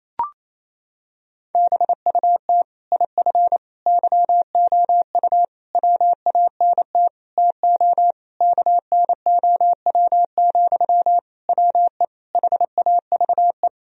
Morse code, heard as a keyed tone, about 28 wpm, 700 Hz. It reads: BUT IF YOU WANT TO KNOW, WE HAVE
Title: Sentences from Top 1000 Words - Review Part 3 - 28wpm